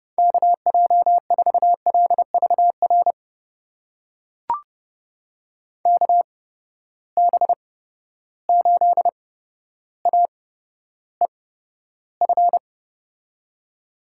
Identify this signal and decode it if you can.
Morse code, heard as a keyed tone, about 30 wpm, spaced as for 13 wpm, 700 Hz. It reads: KJ4LVR KB8AEF
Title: USA Call Signs - 6 Characters - 8x Character Spacing - 30wpm